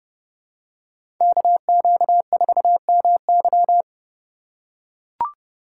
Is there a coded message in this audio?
Yes: Morse code from 1.2 to 3.8 s, heard as a keyed tone, about 30 wpm, 700 Hz. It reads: KQ4MY